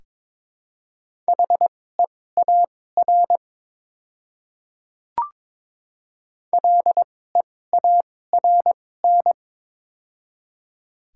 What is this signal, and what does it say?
Morse code, heard as a keyed tone, about 22 wpm, spaced as for 16 wpm, 700 Hz. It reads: HEAR LEARN
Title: 2x Character Spacing - Top 300 Words - 22wpm